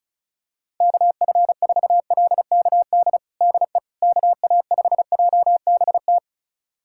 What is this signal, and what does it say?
Morse code, heard as a keyed tone, about 34 wpm, 700 Hz. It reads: KF4LKD DE KA5JBT
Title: USA Callsign DE Callsign - 6 Characters - 35wpm